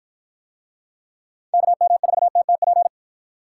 Morse code, heard as a keyed tone, about 48 wpm, 700 Hz. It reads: KM4TTP